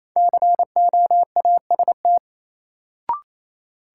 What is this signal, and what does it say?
Morse code, heard as a keyed tone, about 28 wpm, 700 Hz. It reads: COAST
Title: Sentences from Top 1000 Words - Review Part 4 - New Word Drill - 28wpm